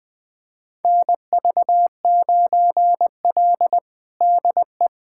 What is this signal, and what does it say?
Morse code, heard as a keyed tone, about 20 wpm, 700 Hz. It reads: NV9L DE
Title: USA Callsign DE Callsign - 4 Characters - 20wpm